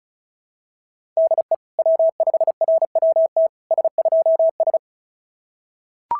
Morse code, heard as a keyed tone, about 36 wpm, 650 Hz. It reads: DE W5RWT S2S
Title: SOTA QSO Practice 35wpm